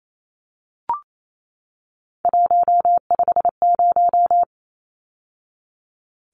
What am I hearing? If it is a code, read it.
Morse code, heard as a keyed tone, about 28 wpm, 700 Hz. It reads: 150
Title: Numbers - 3-Digit 28wpm